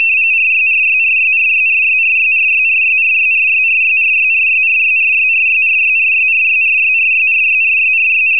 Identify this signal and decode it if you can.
RTTY, about 45.45 baud. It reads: RYRYRYRYRYRYRYRYRYRYRYRYRYRYRYRYRYRYRYRYRYRYRYRYR
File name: k3rtty / AFSK-512-tap 60db short